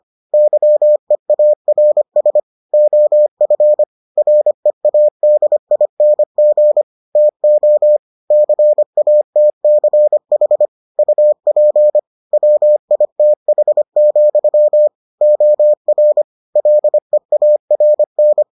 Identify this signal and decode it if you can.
Morse code, heard as a keyed tone, about 25 wpm, 600 Hz. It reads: YEARS OF READING TO CATCH UP WITH, OR LEARN